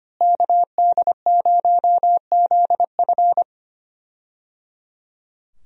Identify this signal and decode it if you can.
Morse code, heard as a keyed tone, about 25 wpm, 700 Hz. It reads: KD0ZF
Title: USA Call Signs - 5 Characters - Repeated 3x - 25wpm